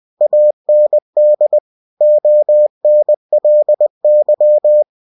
Morse code, heard as a keyed tone, about 20 wpm, 600 Hz. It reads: AND ONLY
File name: cw-books / winnie-the-pooh / winnie-20wpm-page-0023